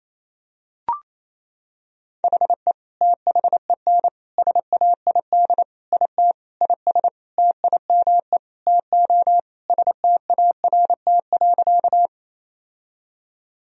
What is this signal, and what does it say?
Morse code, heard as a keyed tone, about 28 wpm, 700 Hz. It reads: HE THEN SAID IT IS TIME TO START.
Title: Sentences from Top 1000 Words - Review Part 1 - 28wpm